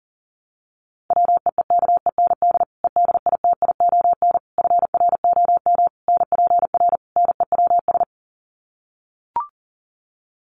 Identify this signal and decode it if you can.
Morse code, heard as a keyed tone, about 40 wpm, 700 Hz. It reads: WEEKEND EDITION FROM NPR NEWS
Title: Sets of 5 Words 40wpm